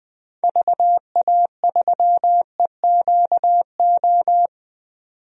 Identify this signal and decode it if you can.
Morse code, heard as a keyed tone, about 20 wpm, 700 Hz. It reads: VA3EQO
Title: DXCC-100 Call Signs - 6 Characters - 20wpm